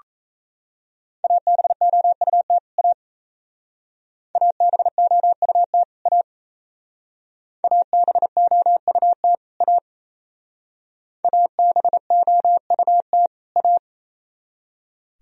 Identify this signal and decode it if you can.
Morse code, heard as a keyed tone, about 35 wpm, 700 Hz. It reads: ABOUT A ABOUT A ABOUT A ABOUT A